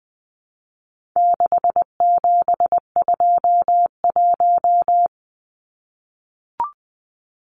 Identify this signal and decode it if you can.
Morse code, heard as a keyed tone, about 20 wpm, 700 Hz. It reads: 6721
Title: Numbers - 4-Digit 20wpm